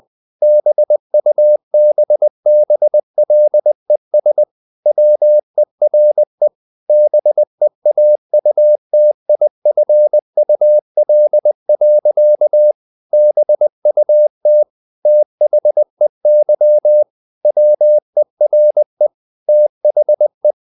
Morse code, heard as a keyed tone, about 20 wpm, 600 Hz. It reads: BUBBLES WERE BEAUTIFUL. BUT THEY WERE THE
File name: the old man and the sea / the old man and the sea part 1 20wpm